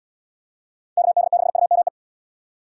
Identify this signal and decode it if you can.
Morse code, heard as a keyed tone, about 69 wpm, 700 Hz. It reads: KD6UME